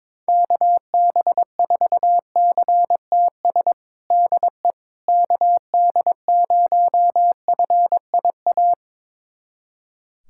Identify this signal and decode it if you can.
Morse code, heard as a keyed tone, about 22 wpm, 700 Hz. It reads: KB4CTS DE KD0FIA
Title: USA Callsign DE Callsign 22wpm